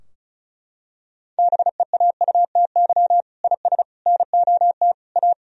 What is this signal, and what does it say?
Morse code, heard as a keyed tone, about 36 wpm, 700 Hz. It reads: BEAUTY IS NOT A
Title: Sentences from Top 1000 Words - Review Part 4 - 35wpm